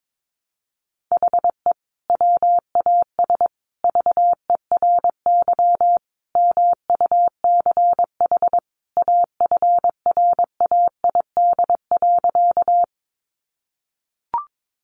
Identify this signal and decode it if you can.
Morse code, heard as a keyed tone, about 22 wpm, 700 Hz. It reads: HE WAS VERY MUCH AFRAID.
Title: Sentences from Top 1500 Words 22wpm